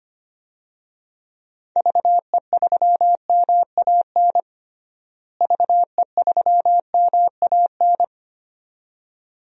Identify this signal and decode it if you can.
Morse code, heard as a keyed tone, about 25 wpm, 700 Hz. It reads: VE3MAN VE3MAN
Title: DXCC-100 Call Signs - 6 Characters - Repeated 2x - 25wpm